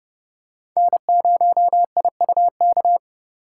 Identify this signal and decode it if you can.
Morse code, heard as a keyed tone, about 31 wpm, 700 Hz. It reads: N0IUK
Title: USA Call Signs - 5 Characters - 4x Character Spacing - 30wpm